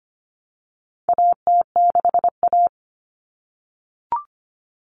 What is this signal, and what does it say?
Morse code, heard as a keyed tone, about 25 wpm, 700 Hz. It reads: AT6A
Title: Common Cut Numbers - 4-Digit 25wpm